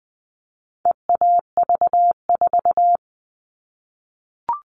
Morse code, heard as a keyed tone, about 20 wpm, 700 Hz. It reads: EAV4
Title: All Cut Numbers - 4-Digit 20wpm